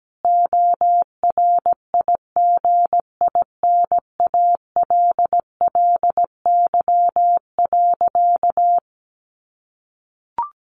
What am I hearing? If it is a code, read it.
Morse code, heard as a keyed tone, about 17 wpm, 700 Hz. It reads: ORIGINALLY.
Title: Sentences from Top 1500 Words 17wpm